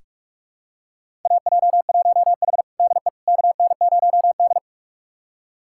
Morse code, heard as a keyed tone, about 46 wpm, 700 Hz. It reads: AJ1H DE KN0D
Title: USA Callsign DE Callsign - 4 Characters - 45wpm